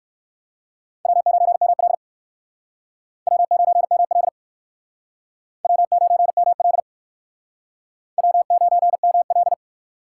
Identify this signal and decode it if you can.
Morse code, heard as a keyed tone, about 59 wpm, 700 Hz. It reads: W9ML W9ML W9ML W9ML